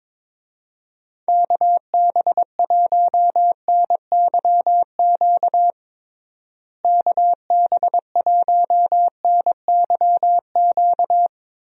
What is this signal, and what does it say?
Morse code, heard as a keyed tone, about 22 wpm, 700 Hz. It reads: KB1NYQ KB1NYQ
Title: USA Call Signs - 6 Characters - Repeated 2x - 22wpm